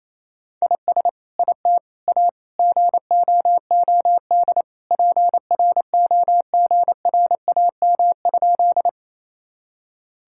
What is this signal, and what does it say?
Morse code, heard as a keyed tone, about 28 wpm, 700 Hz. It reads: IS IT A GOOD PROGRAM?
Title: Sentences from Top 1000 Words - Review Part 5 - 28wpm